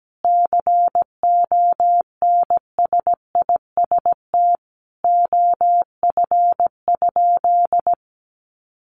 Morse code, heard as a keyed tone, about 17 wpm, 700 Hz. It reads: CONSIST OF?
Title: Sentences from Top 1400 Words 17wpm